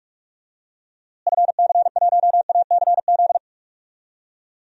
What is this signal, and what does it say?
Morse code, heard as a keyed tone, about 41 wpm, 700 Hz. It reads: FC1ACZ
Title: DXCC-100 Call Signs - 6 Characters - Repeated 2x - 45wpm